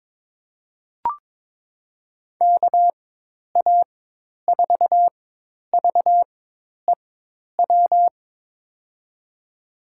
Morse code, heard as a keyed tone, about 22 wpm, spaced as for 10 wpm, 700 Hz. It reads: KA4VEW
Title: USA Call Signs - 6 Characters - 4x Character Spacing - 22wpm